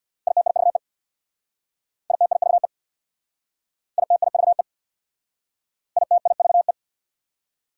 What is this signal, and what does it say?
Morse code, heard as a keyed tone, about 62 wpm, 700 Hz. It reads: ITIVE ITIVE ITIVE ITIVE
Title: Speed-Racing Prefixes and Suffixes 50wpm